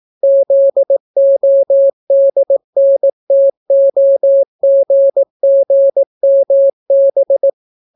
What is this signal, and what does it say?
Morse code, heard as a keyed tone, about 18 wpm, 550 Hz. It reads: ZODNTOGGMB